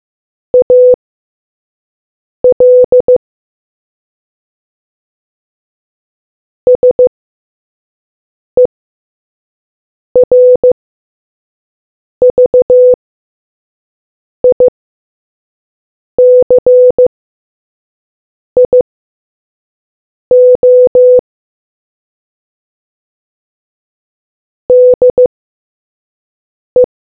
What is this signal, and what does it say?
Morse code, heard as a keyed tone, about 15 wpm, spaced as for 5 wpm, 500 Hz. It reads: AL SERVICIO DE